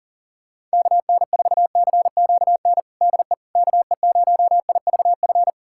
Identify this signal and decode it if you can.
Morse code, heard as a keyed tone, about 40 wpm, 700 Hz. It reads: KN4CQN DE KE0IVF
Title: USA Callsign DE Callsign - 6 Characters - 40wpm